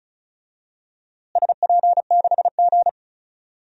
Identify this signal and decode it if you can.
Morse code, heard as a keyed tone, about 34 wpm, 700 Hz. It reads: SP6G
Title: DXCC-100 Call Signs - 4 Characters - Repeated 2x - 35wpm